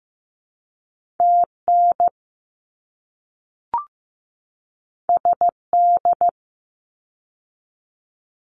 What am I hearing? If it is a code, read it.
Morse code, heard as a keyed tone, about 15 wpm, 700 Hz. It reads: TN SD
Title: ARRL Sections 15wpm